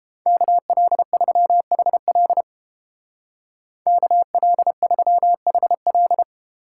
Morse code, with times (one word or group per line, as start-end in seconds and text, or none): 0.3-2.4 KL3HL
3.9-6.2 KL3HL